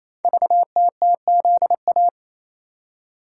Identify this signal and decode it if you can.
Morse code, heard as a keyed tone, about 27 wpm, 700 Hz. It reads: VTTZA
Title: Code Groups - 5 Letters - 28wpm